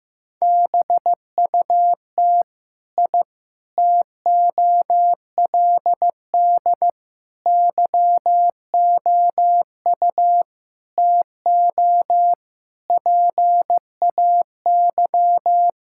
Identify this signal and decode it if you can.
Morse code, heard as a keyed tone, about 15 wpm, 700 Hz. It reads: BUT I TOLD YOU TO PAY
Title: Sentences from Top 1000 Words - Review Part 6 - 15wpm